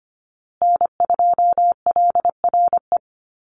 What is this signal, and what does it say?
Morse code, heard as a keyed tone, about 25 wpm, 700 Hz. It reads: N2LRE
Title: USA Call Signs Repeated 3x 25wpm